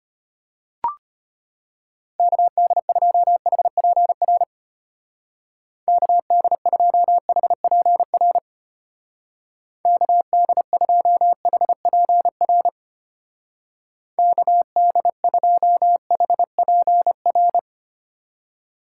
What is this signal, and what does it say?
Morse code, heard as a keyed tone, about 30 wpm, 700 Hz. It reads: KD2HPR KD2HPR KD2HPR KD2HPR
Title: Speed Racing - USA 2x3 Call Signs 25wpm